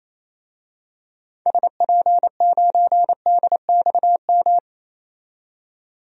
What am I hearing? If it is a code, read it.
Morse code, heard as a keyed tone, about 28 wpm, 700 Hz. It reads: SP9DXM
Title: DXCC-100 Call Signs - 6 Characters - 28wpm